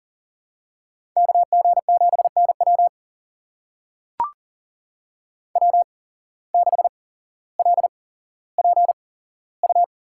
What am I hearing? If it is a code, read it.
Morse code, heard as a keyed tone, about 39 wpm, spaced as for 17 wpm, 700 Hz. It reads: KG7NW W6LPU
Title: USA Call Signs - 5 Characters - 8x Character Spacing - 40wpm